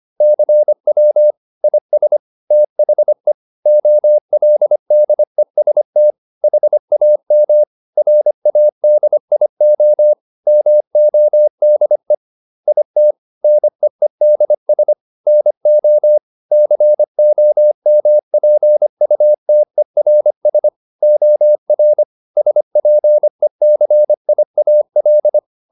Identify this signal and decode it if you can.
Morse code, heard as a keyed tone, about 25 wpm, 600 Hz. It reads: CW IS THE OLDEST HAM RADIO MODE IT NEEDS NO COMPUTERS OR SPECIAL